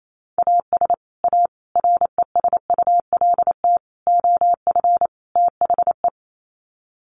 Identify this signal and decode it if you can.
Morse code, heard as a keyed tone, about 28 wpm, 700 Hz. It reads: AS A RESULT OF THE
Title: Sets of 5 Words 28wpm